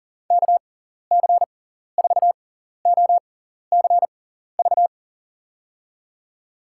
Morse code, heard as a keyed tone, about 40 wpm, spaced as for 14 wpm, 700 Hz. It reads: KC4OCV
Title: USA Call Signs - 6 Characters - 6x Character Spacing - 40wpm